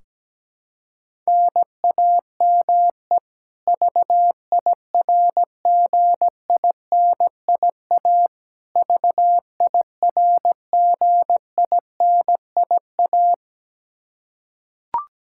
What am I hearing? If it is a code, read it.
Morse code, heard as a keyed tone, about 17 wpm, 700 Hz. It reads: NAME VIRGINIA VIRGINIA
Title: Practice QSO - Exchange 1 - Beginning - 17wpm